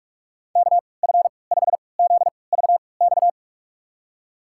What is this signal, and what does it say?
Morse code, heard as a keyed tone, about 46 wpm, spaced as for 26 wpm, 700 Hz. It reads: KF5ZVX
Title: USA Call Signs - 6 Characters - 3x Character Spacing - 45wpm